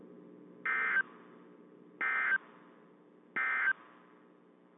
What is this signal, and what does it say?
Emergency Alert System (SAME) tones, heard as harsh buzzing bursts. It is heard from 0.6 to 3.8 s.
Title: WZ2523 - NOAA Weather Radio